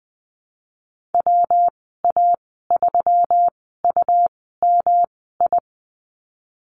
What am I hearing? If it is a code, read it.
Morse code, heard as a keyed tone, about 20 wpm, spaced as for 14 wpm, 700 Hz. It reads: WA3UMI
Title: 122 Lesson 20g - Number 3 - Call Signs 20wpm